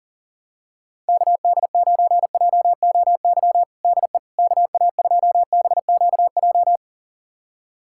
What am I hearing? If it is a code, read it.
Morse code, heard as a keyed tone, about 40 wpm, 700 Hz. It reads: KD9JOY DE KA2BQJ
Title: USA Callsign DE Callsign - 6 Characters - 40wpm